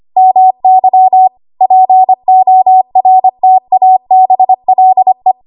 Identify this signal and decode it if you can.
Morse code, heard as a keyed tone, about 25 wpm, 750 Hz. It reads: MY PORTABLE
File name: Archive / 25 WPM / 25WPM